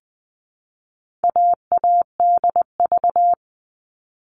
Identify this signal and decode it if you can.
Morse code, heard as a keyed tone, about 20 wpm, 700 Hz. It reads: AADV
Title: All Cut Numbers - 4-Digit 20wpm